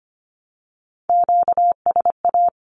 Morse code, heard as a keyed tone, about 25 wpm, 700 Hz. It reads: QSA
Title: Q-Codes Repeated 3x 25wpm